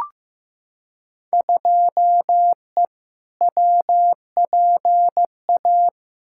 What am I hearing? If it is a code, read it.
Morse code, heard as a keyed tone, about 15 wpm, 700 Hz. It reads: 2E WPA
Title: Field Day Exchanges - 15wpm